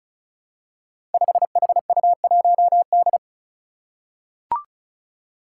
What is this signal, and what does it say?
Morse code, heard as a keyed tone, about 34 wpm, 700 Hz. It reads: 5HU1D
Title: Five Challenging Letter-Number Combinations 35wpm